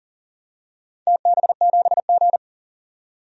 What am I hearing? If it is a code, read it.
Morse code, heard as a keyed tone, about 40 wpm, 690 Hz. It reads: TB7G
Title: DXCC-100 Call Signs - 4 Characters - 40wpm